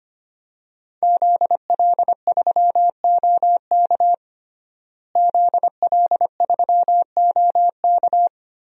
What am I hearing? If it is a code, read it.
Morse code, heard as a keyed tone, about 25 wpm, 700 Hz. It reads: ZL3OK ZL3OK
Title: DXCC-100 Call Signs - 5 Characters - Repeated 2x - 25wpm